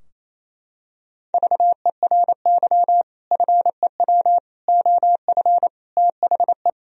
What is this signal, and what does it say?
Morse code, heard as a keyed tone, about 28 wpm, 700 Hz. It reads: VERY FEW OF THE
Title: Sentences from Top 1000 Words - Review Part 3 - 28wpm